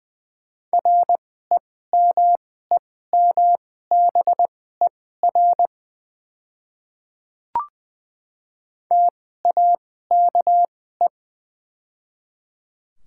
Morse code, heard as a keyed tone, about 20 wpm, spaced as for 14 wpm, 690 Hz. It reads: REMEMBER TAKE
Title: 2x Character Spacing - Top 500 Words - 20wpm